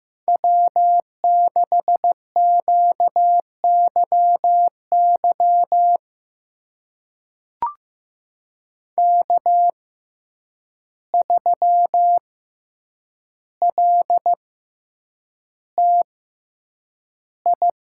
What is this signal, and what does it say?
Morse code, heard as a keyed tone, about 15 wpm, spaced as for 8 wpm, 700 Hz. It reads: W6QYY K3LTI
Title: USA Call Signs - 5 Characters - 6x Character Spacing - 15wpm